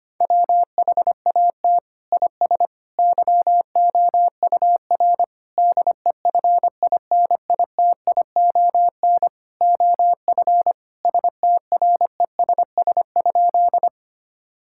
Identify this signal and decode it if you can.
Morse code, heard as a keyed tone, about 25 wpm, 700 Hz. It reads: WHAT IS YOUR DEFINITION OF STRESS?
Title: Sentences from Top 1000 Words - Review Part 6 - 25wpm